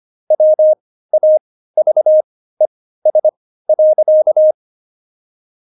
Morse code, heard as a keyed tone, about 25 wpm, spaced as for 15 wpm, 620 Hz. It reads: WAVES.
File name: Jokes-25-char-15-eff